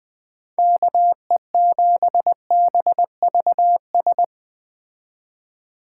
Morse code, heard as a keyed tone, about 20 wpm, 700 Hz. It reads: KE7BVS
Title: Speed Racing - USA Call Signs 17wpm